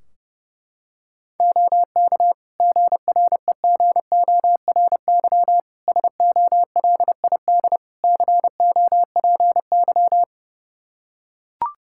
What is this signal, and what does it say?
Morse code, heard as a keyed tone, about 30 wpm, 700 Hz. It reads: OK GREGORY SOLID COPY
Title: Practice QSO - Exchange 3 - More Info - 30wpm